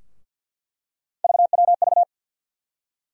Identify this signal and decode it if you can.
Morse code, heard as a keyed tone, about 47 wpm, 700 Hz. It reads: VWV
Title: USA Call Sign Postfixes - 3 Letters 50wpm